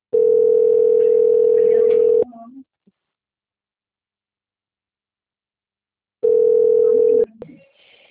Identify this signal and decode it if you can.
Ringback tone (North America), 440 + 480 Hz.
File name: out- - - - - .97947